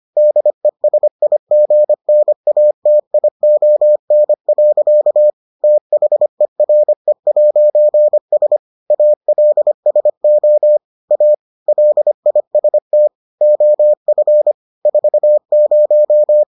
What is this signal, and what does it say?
Morse code, heard as a keyed tone, about 25 wpm, 600 Hz. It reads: DESIGNATION. THERE'S ALSO A LIST OF 40